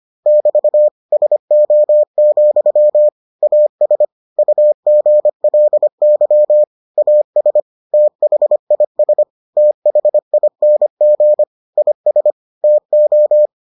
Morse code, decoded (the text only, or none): <BT> SO, AS UGLY AS THIS THING IS TO